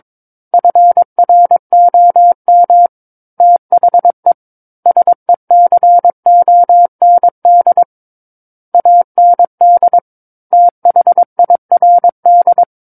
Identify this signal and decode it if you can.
Morse code, heard as a keyed tone, about 22 wpm, 700 Hz. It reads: FROM THE SECOND AND THIRD
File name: Book One / 13-Chapter 13 - 22 WPM